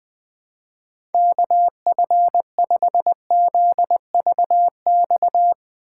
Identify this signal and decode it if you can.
Morse code, heard as a keyed tone, about 21 wpm, 700 Hz. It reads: KF5ZVX
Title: USA Call Signs - 3x Character Spacing - 20wpm